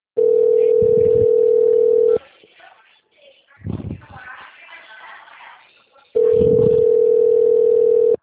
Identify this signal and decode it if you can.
Ringback tone (North America), 440 + 480 Hz.